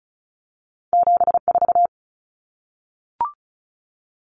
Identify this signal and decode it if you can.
Morse code, heard as a keyed tone, about 36 wpm, 700 Hz. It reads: 74
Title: Common Cut Numbers - 2-Digit 35wpm